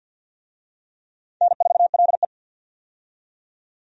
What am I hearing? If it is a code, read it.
Morse code, heard as a keyed tone, about 47 wpm, 700 Hz. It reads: N4LE
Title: USA Call Signs - 4 Characters - 50wpm